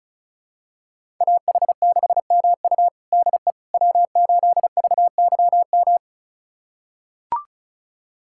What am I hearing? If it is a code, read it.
Morse code, heard as a keyed tone, about 36 wpm, 700 Hz. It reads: AH6MU DE W8VYM
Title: USA Callsign DE Callsign - 5 Characters - 35wpm